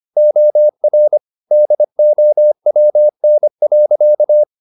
Morse code, heard as a keyed tone, about 25 wpm, 600 Hz. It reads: OR DOWN.
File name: the old man and the sea part 1 25wpm